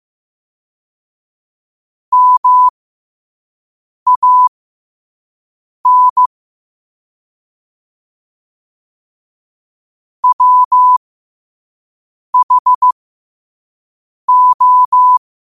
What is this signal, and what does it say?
Morse code, heard as a keyed tone, about 15 wpm, spaced as for 5 wpm, 1000 Hz. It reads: MAN WHO